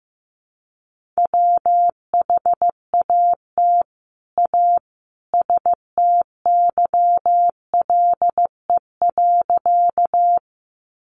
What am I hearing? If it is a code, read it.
Morse code, heard as a keyed tone, about 15 wpm, 700 Hz. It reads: WHAT A STYLE.
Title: Sentences from Top 800 Words 15wpm